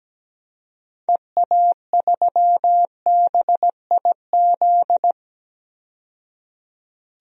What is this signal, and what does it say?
Morse code, heard as a keyed tone, about 17 wpm, 700 Hz. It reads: EA3BIZ
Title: DXCC-100 Call Signs - 6 Characters - 17wpm